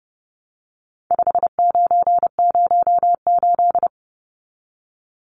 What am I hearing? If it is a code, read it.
Morse code, heard as a keyed tone, about 30 wpm, 700 Hz. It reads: 5908
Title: Numbers - 4-Digit 30wpm